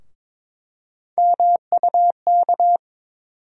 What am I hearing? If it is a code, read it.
Morse code, heard as a keyed tone, about 22 wpm, 700 Hz. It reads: MUK
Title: USA Call Sign Postfixes - 3 Letters 22wpm